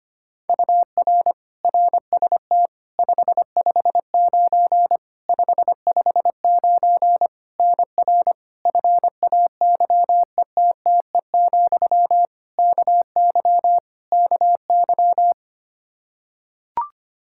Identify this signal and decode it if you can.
Morse code, heard as a keyed tone, about 25 wpm, 700 Hz. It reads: UR RST 559 559 NR FAYETTE, KY KY
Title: Practice QSO - Exchange 1 - Beginning - 25wpm